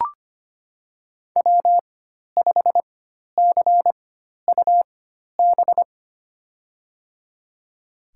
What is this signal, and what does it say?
Morse code, heard as a keyed tone, about 25 wpm, spaced as for 12 wpm, 700 Hz. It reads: W5CUB
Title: USA Call Signs - 5 Characters - 4x Character Spacing - 25wpm